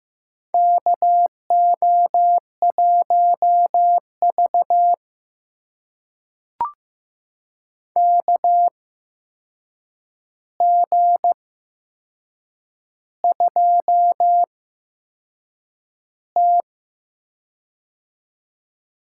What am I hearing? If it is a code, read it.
Morse code, heard as a keyed tone, about 15 wpm, spaced as for 6 wpm, 700 Hz. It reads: KO1V KG2T
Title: USA Call Signs - 4 Characters - 8x Character Spacing - 15wpm